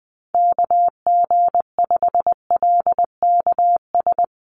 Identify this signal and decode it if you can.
Morse code, heard as a keyed tone, about 20 wpm, 700 Hz. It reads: KG5LKS
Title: USA Call Signs Repeated 3x 20wpm